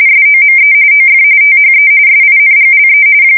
RTTY, about 45.45 baud. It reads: SOME OF THE SINGLE
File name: fsk170 / USS-New-Jersey